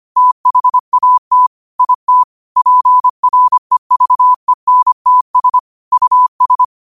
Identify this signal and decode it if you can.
Morse code, heard as a keyed tone, about 25 wpm, 1000 Hz. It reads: THAT IT PREVENTS US
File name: qotdfiles / 25wpm